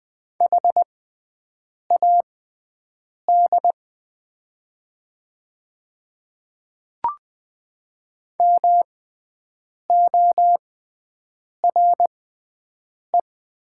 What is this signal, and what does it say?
Morse code, heard as a keyed tone, about 20 wpm, spaced as for 7 wpm, 700 Hz. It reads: HAD MORE
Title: 6x Character Spacing - Top 100 Words - 20wpm